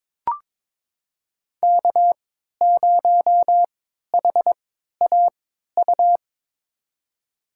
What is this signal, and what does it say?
Morse code, heard as a keyed tone, about 21 wpm, spaced as for 12 wpm, 700 Hz. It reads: K0HAU
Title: USA Call Signs - 5 Characters - 3x Character Spacing - 22wpm